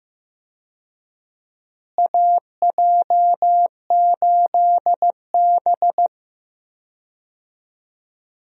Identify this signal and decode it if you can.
Morse code, heard as a keyed tone, about 15 wpm, 700 Hz. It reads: AJ8B
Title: USA Call Signs - 4 Characters - Repeated 2x - 15wpm